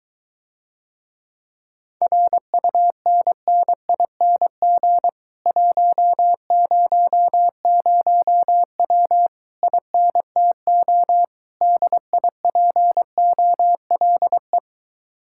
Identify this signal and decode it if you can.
Morse code, heard as a keyed tone, about 23 wpm, 700 Hz. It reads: RUNNING 100W INTO DIPOLE